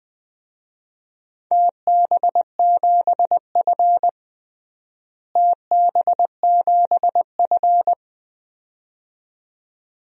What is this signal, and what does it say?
Morse code, heard as a keyed tone, about 20 wpm, 700 Hz. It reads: TB7F TB7F
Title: DXCC-100 Call Signs - 4 Characters - Repeated 2x - 20wpm